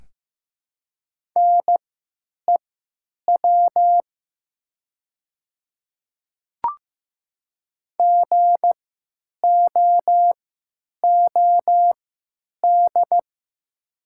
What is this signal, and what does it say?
Morse code, heard as a keyed tone, about 15 wpm, spaced as for 8 wpm, 700 Hz. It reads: NEW GOOD
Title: 3x Character Spacing - Top 300 Words - 15wpm